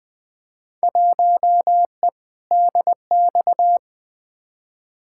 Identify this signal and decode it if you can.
Morse code, heard as a keyed tone, about 20 wpm, 700 Hz. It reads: 1E DX
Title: Field Day Exchanges Encore - 20wpm